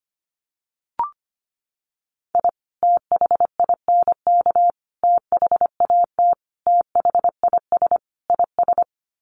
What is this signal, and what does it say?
Morse code, heard as a keyed tone, about 25 wpm, 700 Hz. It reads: I THINK THAT THIS IS
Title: Sets of 5 Words Encore2 25wpm